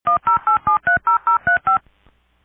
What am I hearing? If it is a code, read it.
DTMF, heard as telephone touch tones, tones of about 120 ms. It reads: 108730035
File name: ring-download